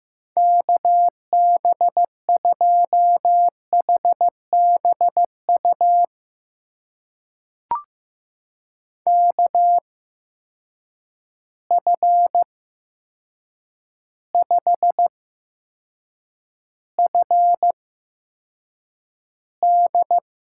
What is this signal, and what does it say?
Morse code, heard as a keyed tone, about 15 wpm, spaced as for 7 wpm, 700 Hz. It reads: KB2HBU KF5FD
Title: USA Call Signs - 6 Characters - 8x Character Spacing - 15wpm